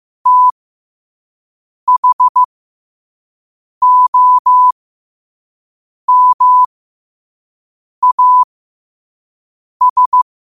Morse code, heard as a keyed tone, about 15 wpm, spaced as for 5 wpm, 1000 Hz. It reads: THOMAS
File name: qotdfiles / 5wpm